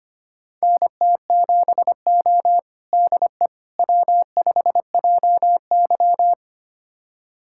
Morse code, heard as a keyed tone, about 25 wpm, 700 Hz. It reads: NT7O DE W5JY
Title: USA Callsign DE Callsign - 4 Characters - 25wpm